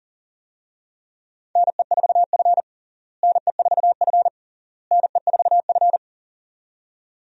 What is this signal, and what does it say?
Morse code, heard as a keyed tone, about 40 wpm, 700 Hz. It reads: NE4F NE4F NE4F
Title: USA Call Signs - 4 Characters - Repeated 3x - 40wpm